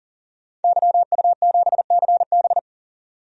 Morse code, heard as a keyed tone, about 37 wpm, 700 Hz. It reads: YU7CB
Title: DXCC-100 Call Signs - 5 Characters - 40wpm